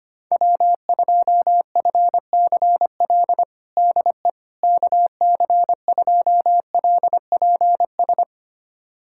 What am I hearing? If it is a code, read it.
Morse code, heard as a keyed tone, about 25 wpm, 700 Hz. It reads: W2FCL DE KC2LPS
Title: USA Callsign DE Callsign 25wpm